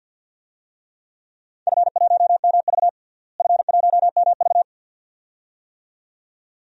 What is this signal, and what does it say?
Morse code, heard as a keyed tone, about 49 wpm, 700 Hz. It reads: F1MV F1MV
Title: DXCC-100 Call Signs - 4 Characters - Repeated 2x - 50wpm